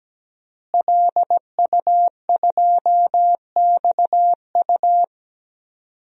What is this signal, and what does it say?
Morse code, heard as a keyed tone, about 17 wpm, 700 Hz. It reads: LU2XU
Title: DXCC-100 Call Signs - 5 Characters - Repeated 2x - 17wpm